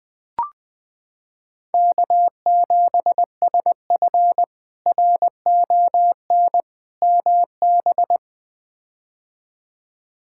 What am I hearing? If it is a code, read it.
Morse code, heard as a keyed tone, about 20 wpm, 700 Hz. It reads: K7SF RON MB
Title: K1USN SST QSO Practice - 20wpm